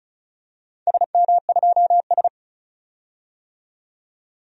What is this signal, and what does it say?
Morse code, heard as a keyed tone, about 36 wpm, 700 Hz. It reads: SM2S
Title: DXCC-100 Call Signs - 4 Characters - 35wpm